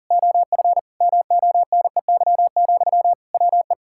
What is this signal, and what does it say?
Morse code, heard as a keyed tone, about 40 wpm, 700 Hz. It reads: OF MONEY, WE
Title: Sentences from Top 1600 Words 40wpm